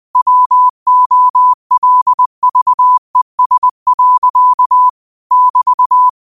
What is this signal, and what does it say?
Morse code, heard as a keyed tone, about 20 wpm, 1000 Hz. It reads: WOLVES. <BT>
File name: qotdfiles / 20wpm